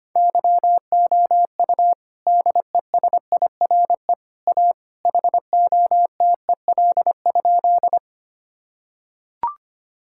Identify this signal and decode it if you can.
Morse code, heard as a keyed tone, about 25 wpm, 700 Hz. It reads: YOU DESIRE A HOTEL?
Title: Sentences from Top 1600 Words 25wpm